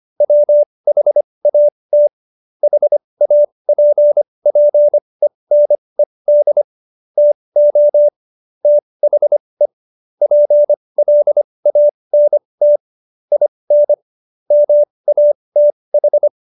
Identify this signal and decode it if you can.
Morse code, heard as a keyed tone, about 25 wpm, spaced as for 20 wpm, 590 Hz. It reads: WHAT HAPPENED TO THE PLANT IN MATH
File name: Jokes-25-char-20-eff